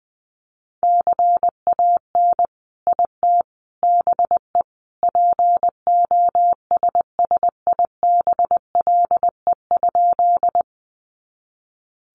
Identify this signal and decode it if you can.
Morse code, heard as a keyed tone, about 20 wpm, 700 Hz. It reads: CAN IT BE POSSIBLE?
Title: Sentences from Top 500 Words 20wpm